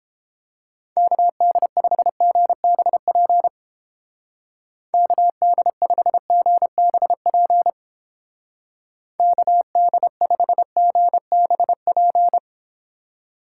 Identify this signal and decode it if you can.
Morse code, heard as a keyed tone, about 29 wpm, 700 Hz. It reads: KD5GBP KD5GBP KD5GBP